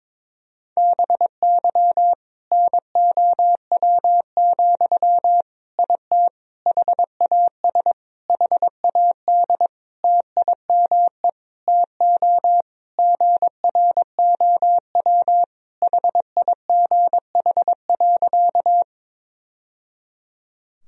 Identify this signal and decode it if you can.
Morse code, heard as a keyed tone, about 22 wpm, 700 Hz. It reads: BY NOW, IT HAS HAD TIME TO GROW HIGH.
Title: Sentences from Top 1000 Words - Review Part 1 - 22wpm